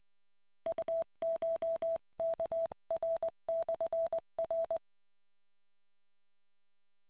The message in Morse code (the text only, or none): UCHKR/R